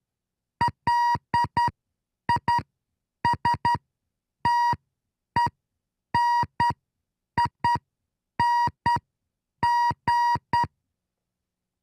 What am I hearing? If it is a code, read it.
Morse code, heard as a keyed tone, about 10 wpm, spaced as for 7 wpm, 950 Hz. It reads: LISTENING